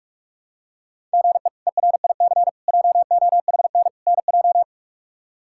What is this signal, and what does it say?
Morse code, heard as a keyed tone, about 41 wpm, 700 Hz. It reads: GE ERIC JOHN NJ
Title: K1USN SST QSO Practice - 45wpm